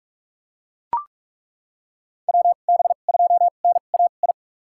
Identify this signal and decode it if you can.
Morse code, heard as a keyed tone, about 42 wpm, spaced as for 31 wpm, 700 Hz. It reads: WB2NAI
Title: USA Call Signs - 2x Character Spacing - 45wpm